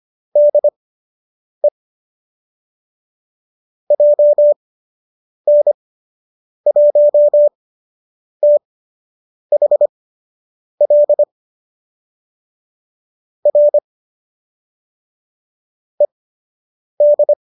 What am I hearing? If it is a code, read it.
Morse code, heard as a keyed tone, about 25 wpm, spaced as for 8 wpm, 570 Hz. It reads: DE JN1THL R ED
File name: qso003 08